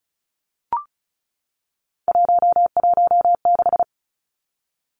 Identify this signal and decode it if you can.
Morse code, heard as a keyed tone, about 34 wpm, 700 Hz. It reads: 116
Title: Numbers - 3-Digit 35wpm